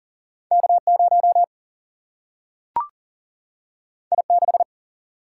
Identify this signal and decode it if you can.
Morse code, heard as a keyed tone, about 39 wpm, 700 Hz. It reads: K 0 I 6
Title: Single-Letter Single-Number 40wpm